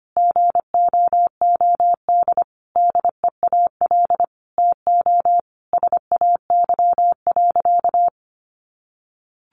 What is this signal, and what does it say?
Morse code, heard as a keyed tone, about 25 wpm, 700 Hz. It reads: GOOD DEAL TO SAY.
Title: Sentences from Top 1000 Words 25wpm